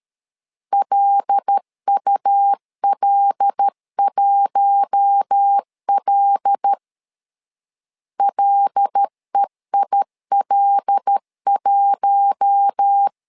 Morse code, heard as a keyed tone, about 13 wpm, 800 Hz. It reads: LUL1L LEIL1
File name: LEZIONI / 5 LBVC / 5 L 12wpm 1913